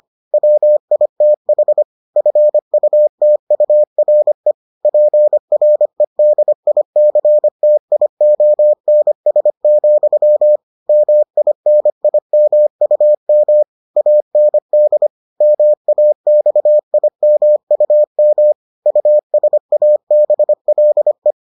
Morse code, heard as a keyed tone, about 25 wpm, 600 Hz. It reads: WITH FUTURE PREDICTIONS, MINIMUM AND MAXIMUM USABLE